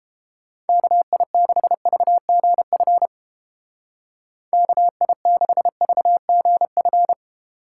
Morse code, heard as a keyed tone, about 32 wpm, 700 Hz. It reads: KI6VGF KI6VGF